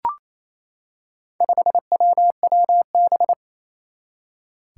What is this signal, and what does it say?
Morse code, heard as a keyed tone, about 28 wpm, 700 Hz. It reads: 5WWB
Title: Four Challenging Letter-Number Combinations 28wpm